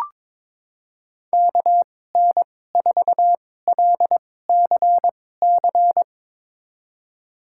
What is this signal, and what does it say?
Morse code, heard as a keyed tone, about 22 wpm, spaced as for 16 wpm, 700 Hz. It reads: KN4LCC
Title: USA Call Signs - 6 Characters - 2x Character Spacing - 22wpm